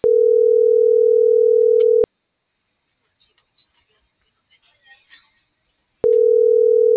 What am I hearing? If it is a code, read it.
Ringback tone (North America), 440 + 480 Hz.
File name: CENTRAL MORENA / out ruth.skyangel.24